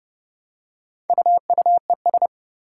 Morse code, heard as a keyed tone, about 29 wpm, 700 Hz. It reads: UUES
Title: Four Challenging Letter-Number Combinations 30wpm